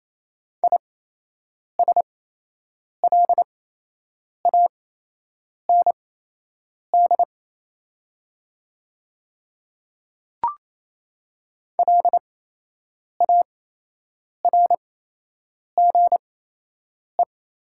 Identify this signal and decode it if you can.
Morse code, heard as a keyed tone, about 28 wpm, spaced as for 8 wpm, 700 Hz. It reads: ISLAND LARGE
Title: 8x Character Spacing - Top 500 Words - 28wpm